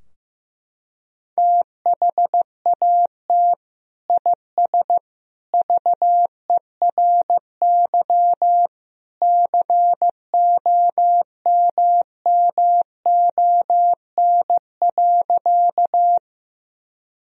Morse code, heard as a keyed tone, about 15 wpm, 700 Hz. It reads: THAT IS VERY COMMON.
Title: Sentences from Top 1000 Words - Review Part 3 - 15wpm